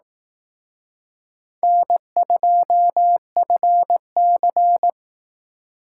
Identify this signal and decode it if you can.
Morse code, heard as a keyed tone, about 18 wpm, 700 Hz. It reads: N2FC
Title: Speed Racing - USA Call Signs 15wpm